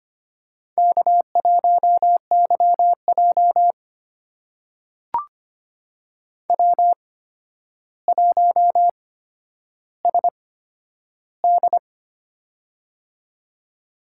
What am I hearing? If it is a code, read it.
Morse code, heard as a keyed tone, about 25 wpm, spaced as for 11 wpm, 700 Hz. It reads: K1YJ W1SD
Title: USA Call Signs - 4 Characters - 8x Character Spacing - 25wpm